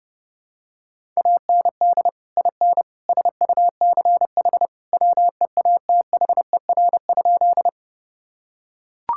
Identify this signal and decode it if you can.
Morse code, heard as a keyed tone, about 30 wpm, 700 Hz. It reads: AND IN SUCH WEATHER?
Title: Sentences from Top 1000 Words - Review Part 5 - 30wpm